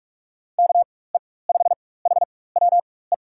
Morse code, heard as a keyed tone, about 44 wpm, spaced as for 21 wpm, 700 Hz. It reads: KE5HWE
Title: USA Call Signs - 6 Characters - 4x Character Spacing - 45wpm